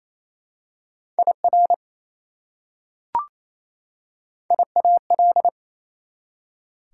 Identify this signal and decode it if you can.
Morse code, heard as a keyed tone, about 28 wpm, 700 Hz. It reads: IR IAL
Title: Prefixes and Suffixes 28wpm